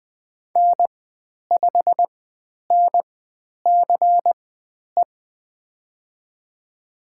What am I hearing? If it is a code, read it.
Morse code, heard as a keyed tone, about 20 wpm, spaced as for 10 wpm, 700 Hz. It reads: N5NCE
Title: 095 Lesson 16g - Letter c - Call Signs 20wpm